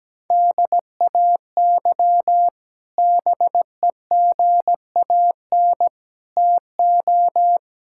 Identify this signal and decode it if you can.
Morse code, heard as a keyed tone, about 17 wpm, 700 Hz. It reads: DAY BEGAN TO
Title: Sentences from Top 1000 Words - Review Part 6 - 17wpm